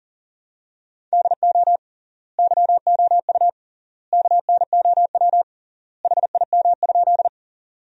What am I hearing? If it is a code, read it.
Morse code, heard as a keyed tone, about 40 wpm, 700 Hz. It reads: DO YOU KNOW HIM?
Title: 3x Word-Spacing - Sentences from Top 100 Words - 40wpm